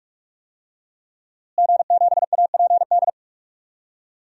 Morse code, heard as a keyed tone, about 46 wpm, 700 Hz. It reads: G7APD
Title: DXCC-100 Call Signs - 5 Characters - Repeated 3x - 45wpm